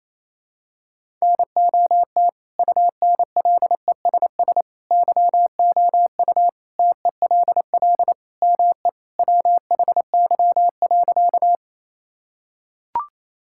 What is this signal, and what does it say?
Morse code, heard as a keyed tone, about 28 wpm, 700 Hz. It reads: NOT UNLESS YOU TELL ME WHY.
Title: Sentences from Top 1000 Words - Review Part 6 - 28wpm